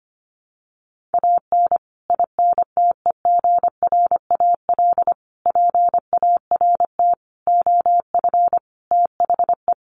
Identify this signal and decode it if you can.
Morse code, heard as a keyed tone, about 25 wpm, 700 Hz. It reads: AN INTEGRAL PART OF THE
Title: Sets of 5 Words 25wpm